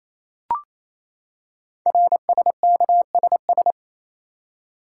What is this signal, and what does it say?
Morse code, heard as a keyed tone, about 27 wpm, 700 Hz. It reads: RSKSS
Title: Code Groups - 5 Letters - 28wpm